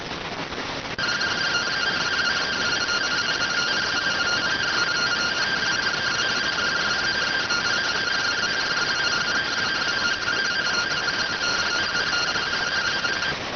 Two-tone teleprinter signal, about 170 Hz shift.